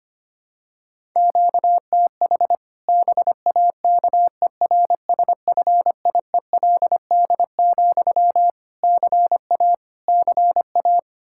Morse code, heard as a keyed tone, about 25 wpm, 700 Hz. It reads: QTH BAKERSFIELD, CA CA
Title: Practice QSO - Exchange 1 - Beginning - 25wpm